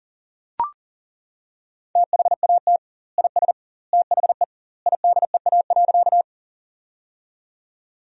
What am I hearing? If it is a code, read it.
Morse code, heard as a keyed tone, about 40 wpm, 700 Hz. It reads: THAT IS THE IDEA.
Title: 2x Word-Spacing - Sentences from Top 300 Words - 40wpm